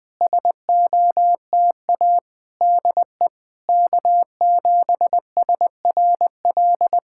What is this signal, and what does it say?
Morse code, heard as a keyed tone, about 20 wpm, 690 Hz. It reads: SOTA DE K7SRL
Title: SOTA QSO Practice 20wpm